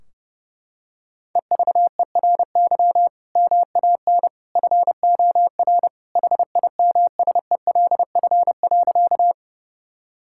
Morse code, heard as a keyed tone, about 30 wpm, 700 Hz. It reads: EVERY MAN FOR HIMSELF.
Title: Sentences from Top 1600 Words 30wpm